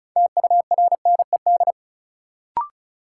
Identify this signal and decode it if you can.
Morse code, heard as a keyed tone, about 36 wpm, 700 Hz. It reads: TURNED
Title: Sets of 4 Words - New Word Drill - 35wpm